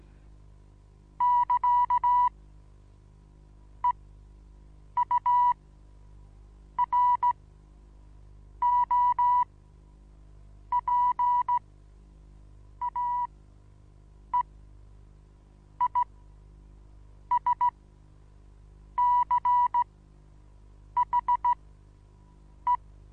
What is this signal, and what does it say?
Morse code, heard as a keyed tone, about 15 wpm, spaced as for 6 wpm, 980 Hz. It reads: <KA> EUROPAEISCHE
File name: Onlinelehrgang / morsen / DJ4UF Morsen25